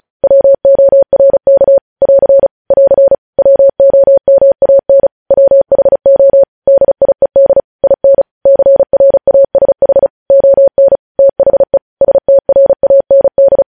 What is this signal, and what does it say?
Morse code, heard as a keyed tone, about 36 wpm, 550 Hz. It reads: WORK <AR> <AR> WOMAN WHO DIED IN CRASH ON THE STRAND
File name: podcast / News headlines-1x-35x35